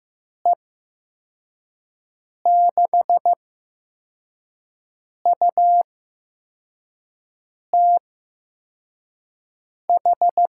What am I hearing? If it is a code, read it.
Morse code, heard as a keyed tone, about 16 wpm, spaced as for 4 wpm, 700 Hz. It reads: E6UTH
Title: USA Call Signs - 6 Characters - 8x Character Spacing - 15wpm